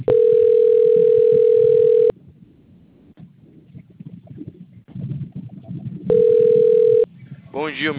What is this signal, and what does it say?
Ringback tone (North America), 440 + 480 Hz.